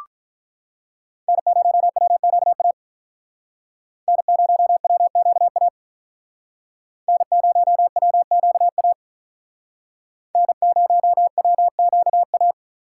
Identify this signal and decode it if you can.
Morse code, heard as a keyed tone, about 46 wpm, 700 Hz. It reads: N0WQA N0WQA N0WQA N0WQA